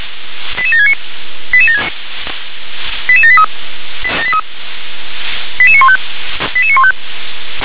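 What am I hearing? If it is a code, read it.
Five-tone selective call (ZVEI), heard as quick runs of five tones, tones of about 70 ms. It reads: 88769 88783 88915 88915
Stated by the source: ZVEI five-tone call received with an SDR.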